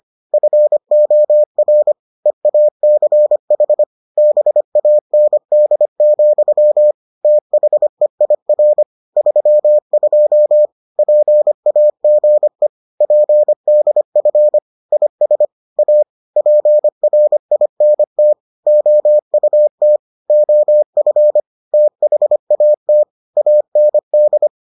Morse code, heard as a keyed tone, about 25 wpm, 600 Hz. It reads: FOR EACH BAND, THEIR 32 PAGE PDF IS A PRINT OUT OF THAT AND